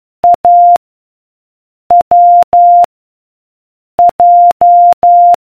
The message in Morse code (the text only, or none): A W J